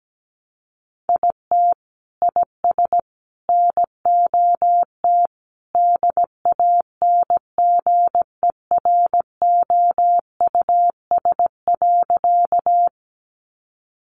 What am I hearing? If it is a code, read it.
Morse code, heard as a keyed tone, about 17 wpm, 700 Hz. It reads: IT IS NOT DANGEROUS.
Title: Sentences from Top 1200 Words 17wpm